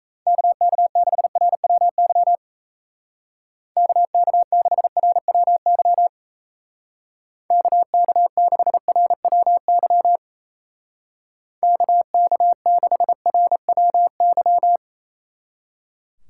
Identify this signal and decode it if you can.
Morse code, heard as a keyed tone, about 34 wpm, 700 Hz. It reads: KK6RWY KK6RWY KK6RWY KK6RWY
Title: Speed Racing - USA Call Signs 28wpm